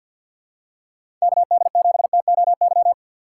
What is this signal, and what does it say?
Morse code, heard as a keyed tone, about 53 wpm, 700 Hz. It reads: KD7TOY
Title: USA Call Signs - 6 Characters - 3x Character Spacing - 50wpm